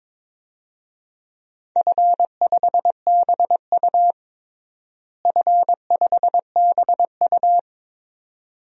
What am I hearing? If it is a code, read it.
Morse code, heard as a keyed tone, about 22 wpm, 700 Hz. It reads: F5BU F5BU
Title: DXCC-100 Call Signs - 4 Characters - Repeated 2x - 22wpm